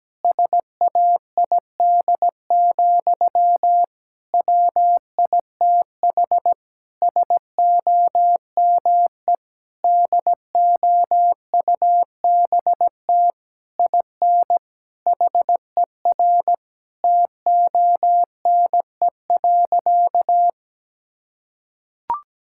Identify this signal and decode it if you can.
Morse code, heard as a keyed tone, about 17 wpm, 700 Hz. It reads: SAID, WITH SOME DOUBT IN HER TONE.
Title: Sentences from Top 1600 Words 17wpm